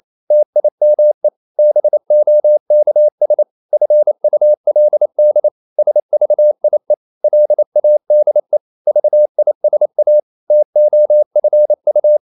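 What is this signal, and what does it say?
Morse code, heard as a keyed tone, about 28 wpm, 600 Hz. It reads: TIME BOKS FULD SVIE LADE VISA TOFU
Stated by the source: Danish short words sent in Morse at 28 wpm, from CW med Gnister.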